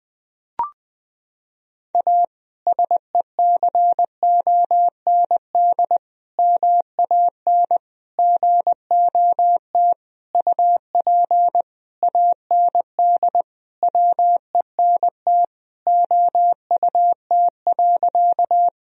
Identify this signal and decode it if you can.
Morse code, heard as a keyed tone, about 20 wpm, 700 Hz. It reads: A SECOND MAN GOT UP AND WENT OUT.
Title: Sentences from Top 1000 Words - Review Part 1 - 20wpm